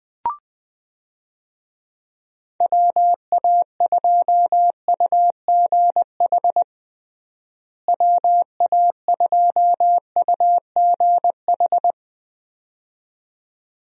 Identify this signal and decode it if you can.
Morse code, heard as a keyed tone, about 20 wpm, 700 Hz. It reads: WA2UGH WA2UGH
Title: USA Call Signs - 6 Characters - Repeated 2x - 20wpm